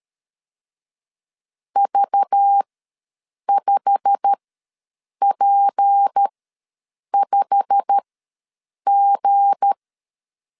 Morse code, heard as a keyed tone, about 13 wpm, spaced as for 7 wpm, 800 Hz. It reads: V5P5G
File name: LEZIONI / 8 YJ45 / 8 YJ45 ws 12wpm 3146